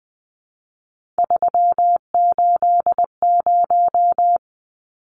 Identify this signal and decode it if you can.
Morse code, heard as a keyed tone, about 20 wpm, 700 Hz. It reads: 380
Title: Numbers - 3-Digit 20wpm